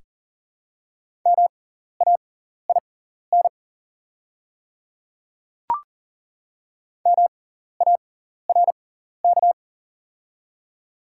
Morse code, heard as a keyed tone, about 40 wpm, spaced as for 14 wpm, 690 Hz. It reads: MAIN MARK
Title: 6x Character Spacing - Top 300 Words - 40wpm